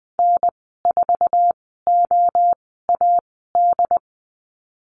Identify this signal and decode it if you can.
Morse code, heard as a keyed tone, about 20 wpm, spaced as for 15 wpm, 700 Hz. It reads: N4OAD
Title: 088 Lesson 15g - Number 5 - Call Signs 20wpm